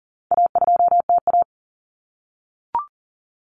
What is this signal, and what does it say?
Morse code, heard as a keyed tone, about 40 wpm, 700 Hz. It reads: A2TA
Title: Common Cut Numbers - 4-Digit 40wpm